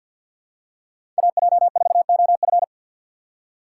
Morse code, heard as a keyed tone, about 53 wpm, 700 Hz. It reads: A J 4 O F
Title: USA Call Signs - 5 Characters - 6x Character Spacing - 50wpm